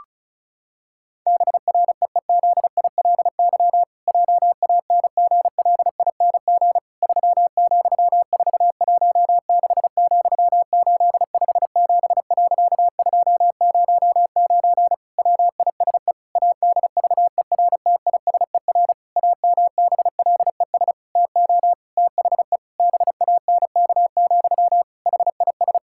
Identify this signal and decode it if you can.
Morse code, heard as a keyed tone, about 36 wpm, 700 Hz. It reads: BREEZILY JANGLING 3,416,857.209 WISE ADVERTISER AMBLES TO THE BANK, HIS
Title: Pangram Challenge v2 - 35wpm